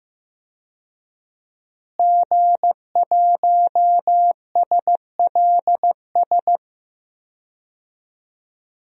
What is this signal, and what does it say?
Morse code, heard as a keyed tone, about 15 wpm, 700 Hz. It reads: G1SLS
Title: DXCC-100 Call Signs - 5 Characters - Repeated 2x - 15wpm